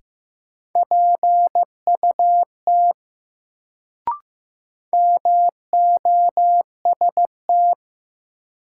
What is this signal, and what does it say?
Morse code, heard as a keyed tone, about 15 wpm, 700 Hz. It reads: PUT MOST
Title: Sentences from Top 200 Words - New Word Drill - 15wpm